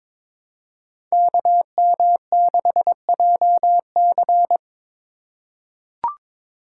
Morse code, heard as a keyed tone, about 22 wpm, 700 Hz. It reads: KM6JC
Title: Code Groups - 5 Letter:Numbers - 22wpm